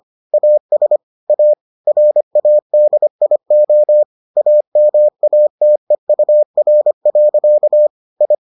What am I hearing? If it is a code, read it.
Morse code, heard as a keyed tone, about 25 wpm, 600 Hz. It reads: AS A RADIO AMATEUR. I